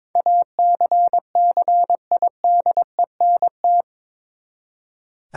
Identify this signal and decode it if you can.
Morse code, heard as a keyed tone, about 21 wpm, 700 Hz. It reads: ACCIDENT
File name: practice / 22wpm / Eight Letter Words - 22wpm